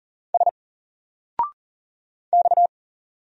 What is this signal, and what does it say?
Morse code, heard as a keyed tone, about 40 wpm, 700 Hz. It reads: S X
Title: Single Letter-Number 40wpm